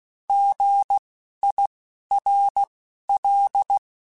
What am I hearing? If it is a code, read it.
Morse code, heard as a keyed tone, about 16 wpm, spaced as for 11 wpm, 800 Hz. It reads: GIRL
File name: Alphabetical / Alphabetical mp3s / Gs @10